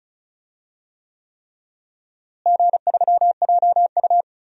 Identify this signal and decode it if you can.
Morse code, heard as a keyed tone, about 34 wpm, 700 Hz. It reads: G3JU
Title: DXCC-100 Call Signs - 4 Characters - Repeated 3x - 35wpm